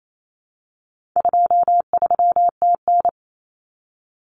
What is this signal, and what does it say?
Morse code, heard as a keyed tone, about 28 wpm, 700 Hz. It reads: 23TN
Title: Common Cut Numbers - 4-Digit 28wpm